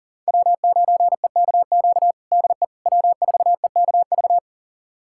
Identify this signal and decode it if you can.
Morse code, heard as a keyed tone, about 40 wpm, 700 Hz. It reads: W9EKQ DE W4EKV
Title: USA Callsign DE Callsign - 5 Characters - 40wpm